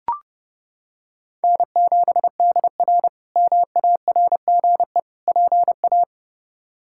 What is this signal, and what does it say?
Morse code, heard as a keyed tone, about 32 wpm, 700 Hz. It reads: N7DR MARGE PA
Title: K1USN SST QSO Practice - 30wpm